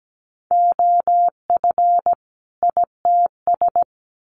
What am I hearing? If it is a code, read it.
Morse code, heard as a keyed tone, about 17 wpm, 700 Hz. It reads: OF ITS
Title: Sets of 2 Words 17wpm